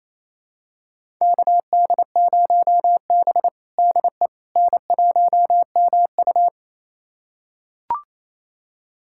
Morse code, heard as a keyed tone, about 28 wpm, 700 Hz. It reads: KD0B DE N1MU
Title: USA Callsign DE Callsign - 4 Characters - 28wpm